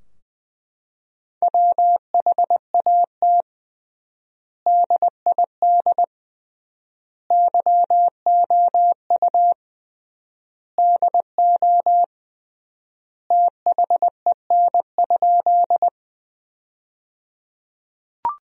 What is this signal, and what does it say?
Morse code, heard as a keyed tone, about 20 wpm, 700 Hz. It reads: WHAT DID YOU DO THEN?
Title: 3x Word-Spacing - Sentences from Top 100 Words - 20wpm